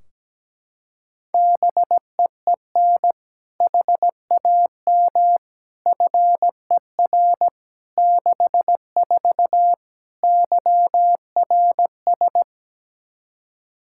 Morse code, heard as a keyed tone, about 17 wpm, 700 Hz. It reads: BEEN HAM FER 64 YRS
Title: K1USN SST QSO Practice 17wpm